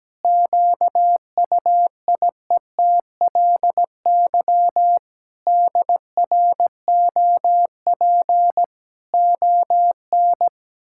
Morse code, heard as a keyed tone, about 17 wpm, 700 Hz. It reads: QUIETLY DROP ON
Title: Pangram Challenge v2 - 17wpm